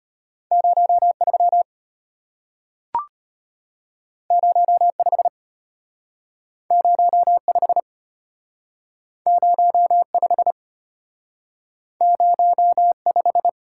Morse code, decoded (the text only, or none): 03 05 05 05 05